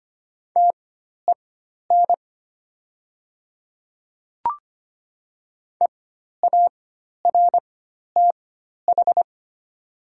Morse code, heard as a keyed tone, about 25 wpm, spaced as for 12 wpm, 700 Hz. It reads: TEN EARTH
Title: 4x Character Spacing - Top 500 Words - 25wpm